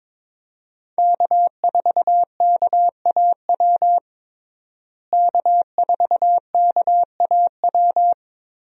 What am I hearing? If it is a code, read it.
Morse code, heard as a keyed tone, about 22 wpm, 700 Hz. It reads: K4KAW K4KAW
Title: USA Call Signs - 5 Characters - Repeated 2x - 22wpm